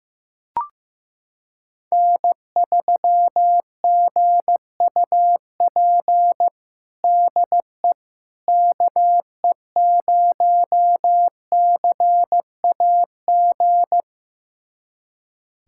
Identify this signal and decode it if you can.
Morse code, heard as a keyed tone, about 15 wpm, 700 Hz. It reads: N3GUP DE KE0CAG
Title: USA Callsign DE Callsign 15wpm